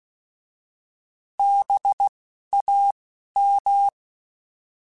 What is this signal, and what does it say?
Morse code, heard as a keyed tone, about 16 wpm, spaced as for 11 wpm, 800 Hz. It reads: BAM